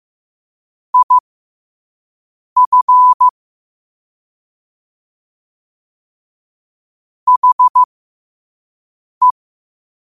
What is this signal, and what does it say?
Morse code, heard as a keyed tone, about 15 wpm, spaced as for 5 wpm, 1000 Hz. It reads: IF HE
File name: qotdfiles / 5wpm